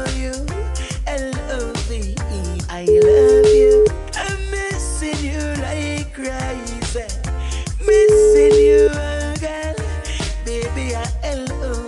Ringback tone (Europe, 425 Hz).